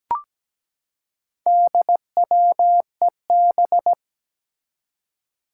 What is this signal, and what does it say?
Morse code, heard as a keyed tone, about 17 wpm, 700 Hz. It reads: DWEB
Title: Four Challenging Letter-Number Combinations 17wpm